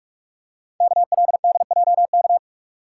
Morse code, heard as a keyed tone, about 44 wpm, 700 Hz. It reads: KLDJK
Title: Code Groups - 5 Letters - 45wpm